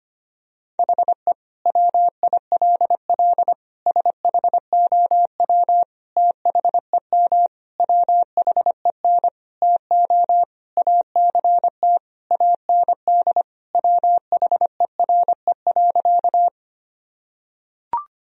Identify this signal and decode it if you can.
Morse code, heard as a keyed tone, about 25 wpm, 700 Hz. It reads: HE WILL SHOW THEM WHEN TO ACT AND WHERE.
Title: Sentences from Top 1000 Words - Review Part 1 - 25wpm